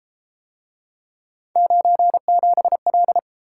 Morse code, heard as a keyed tone, about 33 wpm, 700 Hz. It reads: 97L